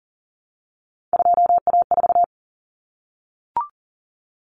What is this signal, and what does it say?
Morse code, heard as a keyed tone, about 40 wpm, 700 Hz. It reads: 2A4
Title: Common Cut Numbers - 3-Digit 40wpm